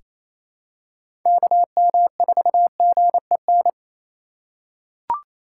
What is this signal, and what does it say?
Morse code, heard as a keyed tone, about 28 wpm, 700 Hz. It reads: KM4GEN
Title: USA Call Signs - 3x Character Spacing - 28wpm